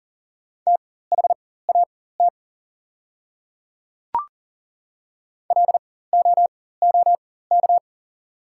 Morse code, heard as a keyed tone, about 40 wpm, spaced as for 19 wpm, 690 Hz. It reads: THAT LOOK
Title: 4x Character Spacing - Top 100 Words - 40wpm